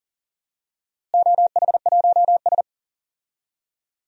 Morse code, heard as a keyed tone, about 40 wpm, 700 Hz. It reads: OH1S
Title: DXCC-100 Call Signs - 4 Characters - Repeated 3x - 40wpm